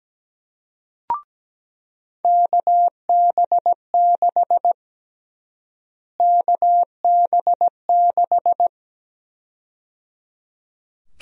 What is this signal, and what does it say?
Morse code, heard as a keyed tone, about 17 wpm, 700 Hz. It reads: KB6 KB6
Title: USA Call Sign Prefixes - Repeated 2x - 17wpm